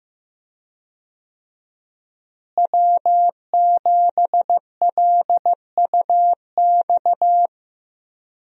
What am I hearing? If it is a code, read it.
Morse code, heard as a keyed tone, about 15 wpm, 700 Hz. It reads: W7LUX
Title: USA Call Signs - 5 Characters - Repeated 3x - 15wpm